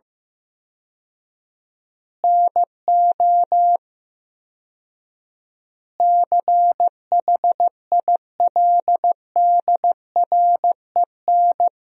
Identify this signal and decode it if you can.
Morse code, heard as a keyed tone, about 14 wpm, 700 Hz. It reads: NO CHILDREN
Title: 4x Word-Spacing - Sentences from Top 400 Words - 15wpm